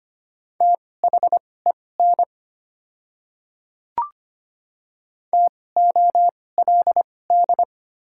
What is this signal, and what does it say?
Morse code, heard as a keyed tone, about 25 wpm, spaced as for 18 wpm, 700 Hz. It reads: THEN TOLD
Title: 2x Character Spacing - Top 500 Words - 25wpm